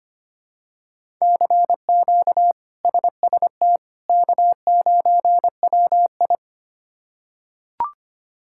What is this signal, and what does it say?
Morse code, heard as a keyed tone, about 25 wpm, 700 Hz. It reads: CQ SST K9WI
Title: K1USN SST QSO Practice - 25wpm